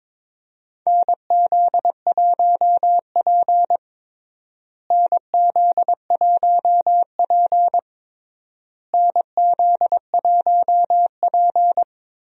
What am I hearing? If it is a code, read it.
Morse code, heard as a keyed tone, about 22 wpm, 700 Hz. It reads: NZ1P NZ1P NZ1P
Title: USA Call Signs - 4 Characters - Repeated 3x - 22wpm